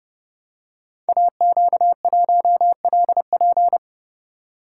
Morse code, heard as a keyed tone, about 30 wpm, 700 Hz. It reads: AQ1LP
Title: Code Groups - 5 Letter:Numbers - 30wpm